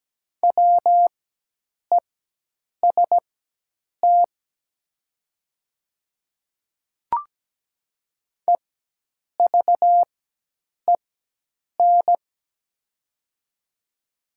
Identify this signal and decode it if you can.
Morse code, heard as a keyed tone, about 17 wpm, spaced as for 8 wpm, 700 Hz. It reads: WEST EVEN
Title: 4x Character Spacing - Top 500 Words - 17wpm